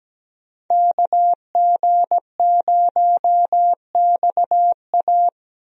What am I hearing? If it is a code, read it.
Morse code, heard as a keyed tone, about 17 wpm, 700 Hz. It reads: KG0XA
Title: USA Call Signs - 5 Characters - 17wpm